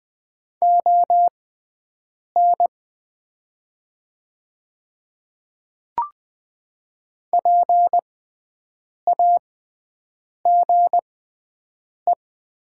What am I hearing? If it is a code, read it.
Morse code, heard as a keyed tone, about 20 wpm, spaced as for 7 wpm, 700 Hz. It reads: ON PAGE
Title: 6x Character Spacing - Top 300 Words - 20wpm